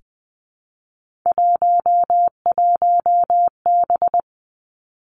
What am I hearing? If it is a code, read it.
Morse code, heard as a keyed tone, about 20 wpm, 700 Hz. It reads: 11B
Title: Three Challenging Letter-Number Combinations 20wpm